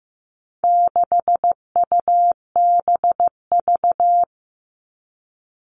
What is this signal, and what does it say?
Morse code, heard as a keyed tone, about 15 wpm, 700 Hz. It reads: 6UBV
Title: All Cut Numbers - 4-Digit 15wpm